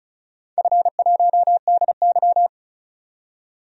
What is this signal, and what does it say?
Morse code, heard as a keyed tone, about 36 wpm, 700 Hz. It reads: F1DY
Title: DXCC-100 Call Signs - 4 Characters - 35wpm